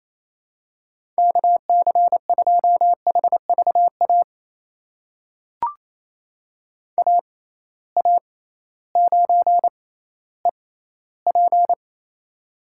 Morse code, heard as a keyed tone, about 28 wpm, spaced as for 15 wpm, 700 Hz. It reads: KC2HVA AA9EP
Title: USA Call Signs - 6x Character Spacing - 28wpm